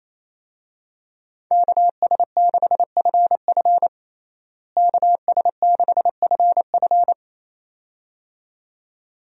Morse code, heard as a keyed tone, about 28 wpm, 700 Hz. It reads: KS6FF KS6FF
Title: USA Call Signs - 5 Characters - Repeated 2x - 28wpm